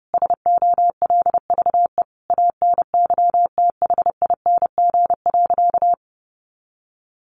Morse code, heard as a keyed tone, about 30 wpm, 700 Hz. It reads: SOLVE ANYTHING.
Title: Sentences from Top 1000 Words 30wpm